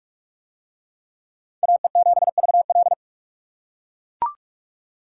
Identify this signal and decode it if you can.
Morse code, heard as a keyed tone, about 42 wpm, 700 Hz. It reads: AE7VL